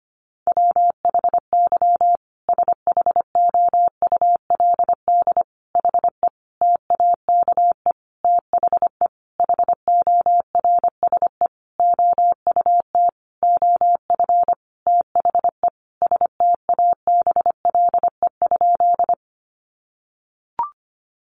Morse code, heard as a keyed tone, about 25 wpm, 700 Hz. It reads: WHY SHOULD HE TAKE THE HORSE OUT OF THE STABLE?
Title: Sentences from Top 1500 Words 25wpm